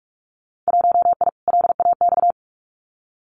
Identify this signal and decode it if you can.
Morse code, heard as a keyed tone, about 46 wpm, 700 Hz. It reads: 1I LAX
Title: Winter Field Day Exchanges 45wpm